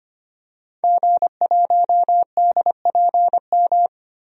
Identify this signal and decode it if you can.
Morse code, heard as a keyed tone, about 25 wpm, 700 Hz. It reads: G1DPM
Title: DXCC-100 Call Signs - 5 Characters - Repeated 3x - 25wpm